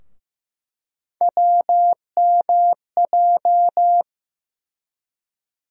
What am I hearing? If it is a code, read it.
Morse code, heard as a keyed tone, about 15 wpm, 700 Hz. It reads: WMJ
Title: USA Call Sign Postfixes - 3 Letters 15wpm